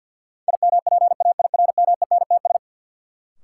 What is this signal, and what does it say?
Morse code, heard as a keyed tone, about 53 wpm, 700 Hz. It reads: IMPAIRMENTS
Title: Sets of 5 Words Encore - New Word Drill - 50wpm